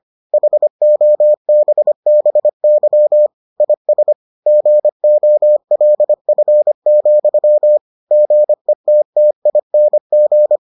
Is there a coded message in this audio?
Yes: Morse code from 0.3 to 10.6 s, heard as a keyed tone, about 25 wpm, 600 Hz. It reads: HOBBY IS GOLF, GETTING